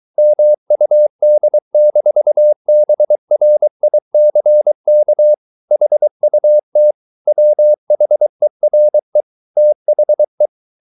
Morse code, heard as a keyed tone, about 23 wpm, 600 Hz. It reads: MUD-BRICK HUT WHERE THE